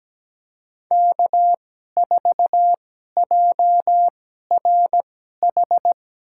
Morse code, heard as a keyed tone, about 17 wpm, spaced as for 12 wpm, 700 Hz. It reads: K4JRH
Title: USA Call Signs - 5 Characters - 2x Character Spacing - 17wpm